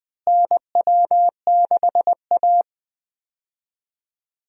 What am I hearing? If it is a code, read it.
Morse code, heard as a keyed tone, about 20 wpm, 700 Hz. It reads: NW6A
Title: USA Call Signs - 4 Characters - Repeated 2x - 20wpm